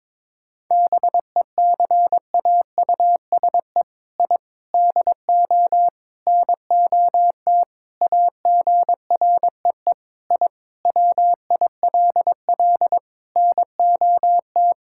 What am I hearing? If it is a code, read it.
Morse code, heard as a keyed tone, about 22 wpm, 700 Hz. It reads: BECAUSE I DO NOT AGREE I WILL NOT
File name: practice / 22wpm / Sentences with 4-Word Phrases - 22wpm